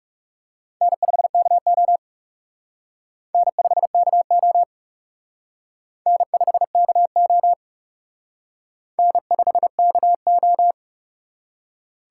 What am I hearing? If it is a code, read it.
Morse code, heard as a keyed tone, about 37 wpm, 700 Hz. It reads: N5KO N5KO N5KO N5KO
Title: Speed Racing - USA 1x2 Call Signs 30wpm